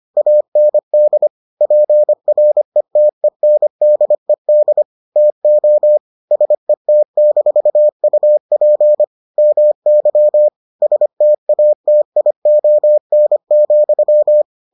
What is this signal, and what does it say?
Morse code, heard as a keyed tone, about 25 wpm, 600 Hz. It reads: AND PRETENDED TO SET-UP MY STATION,